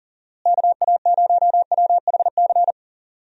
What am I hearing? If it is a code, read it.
Morse code, heard as a keyed tone, about 37 wpm, 700 Hz. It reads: KA0WHC
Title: USA Call Signs - 6 Characters - Repeated 2x - 40wpm